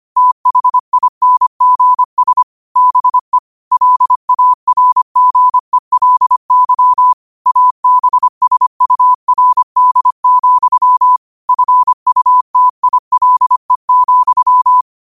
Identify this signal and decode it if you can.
Morse code, heard as a keyed tone, about 25 wpm, 1000 Hz. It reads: THINGS BE LARGELY ABSURD, FUTILE,
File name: qotdfiles / 25wpm